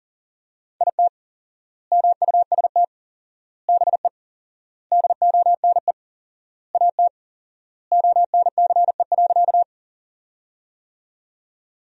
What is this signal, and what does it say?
Morse code, heard as a keyed tone, about 40 wpm, 700 Hz. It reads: IT MUST BE DONE AT ONCE.
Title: 4x Word-Spacing - Sentences from Top 500 Words - 40wpm